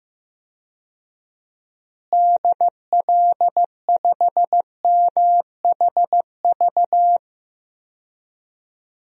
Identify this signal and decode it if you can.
Morse code, heard as a keyed tone, about 15 wpm, 700 Hz. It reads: DL5MHV
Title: DXCC-100 Call Signs - 6 Characters - Repeated 2x - 15wpm